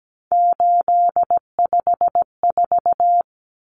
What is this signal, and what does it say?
Morse code, heard as a keyed tone, about 17 wpm, 700 Hz. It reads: 854
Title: Common Cut Numbers - 3-Digit 17wpm